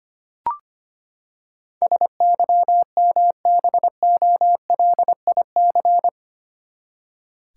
Morse code, heard as a keyed tone, about 25 wpm, 700 Hz. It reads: SYMBOLIC
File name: Eight Letter Words - 25wpm